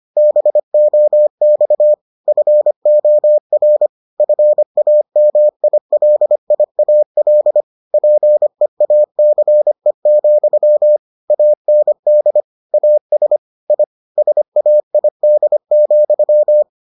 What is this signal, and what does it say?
Morse code, heard as a keyed tone, about 25 wpm, 600 Hz. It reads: BOX FOR FAMILIAL PEACE, AND AS I SAID,